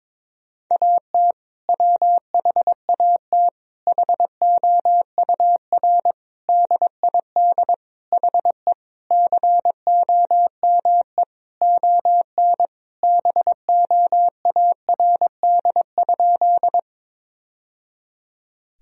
Morse code, heard as a keyed tone, about 22 wpm, 700 Hz. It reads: AT WHAT HOUR DID HE COME ON BOARD?
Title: Sentences from Top 1000 Words - Review Part 4 - 22wpm